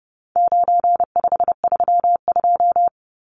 Morse code, heard as a keyed tone, about 30 wpm, 700 Hz. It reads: 9532
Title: Numbers - 4-Digit 30wpm